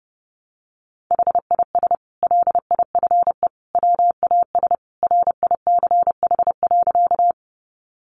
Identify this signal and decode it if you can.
Morse code, heard as a keyed tone, about 30 wpm, 700 Hz. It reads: HIS LIFE WAS RICH.
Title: Sentences from Top 1100 Words 30wpm